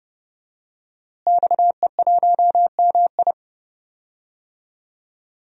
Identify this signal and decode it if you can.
Morse code, heard as a keyed tone, about 32 wpm, 700 Hz. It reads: XE1MI
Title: DXCC-100 Call Signs - 5 Characters - Repeated 2x - 30wpm